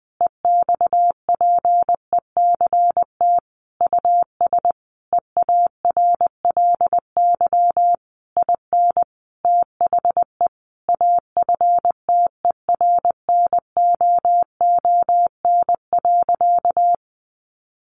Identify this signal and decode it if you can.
Morse code, heard as a keyed tone, about 20 wpm, 700 Hz. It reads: EXPECT US EARLY IN THE AFTERNOON.
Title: Sentences from Top 1500 Words 20wpm